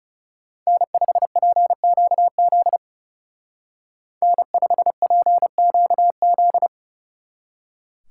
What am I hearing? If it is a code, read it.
Morse code, heard as a keyed tone, about 40 wpm, 700 Hz. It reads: N5PQZ N5PQZ